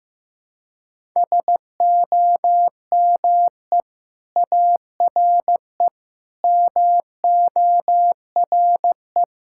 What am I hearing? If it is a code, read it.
Morse code, heard as a keyed tone, about 16 wpm, 700 Hz. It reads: SOME ARE MORE
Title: Sentences from Top 1000 Words - Review Part 6 - 15wpm